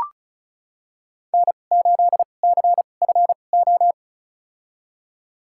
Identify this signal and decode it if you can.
Morse code, heard as a keyed tone, about 36 wpm, spaced as for 25 wpm, 700 Hz. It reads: N8CFO
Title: USA Call Signs - 5 Characters - 2x Character Spacing - 35wpm